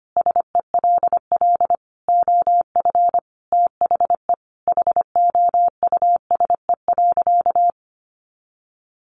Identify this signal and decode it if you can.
Morse code, heard as a keyed tone, about 25 wpm, 700 Hz. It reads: SELL OF THE HOUSE.
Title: Sentences from Top 1200 Words 25wpm